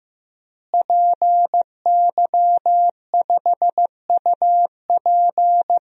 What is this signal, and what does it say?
Morse code, heard as a keyed tone, about 15 wpm, 700 Hz. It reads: PY5UP
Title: DXCC-100 Call Signs - 5 Characters - Repeated 3x - 15wpm